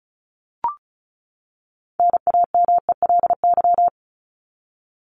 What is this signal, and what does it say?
Morse code, heard as a keyed tone, about 34 wpm, 700 Hz. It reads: NAMELY
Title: LY Suffix - Top 100 Words with Base Words 35wpm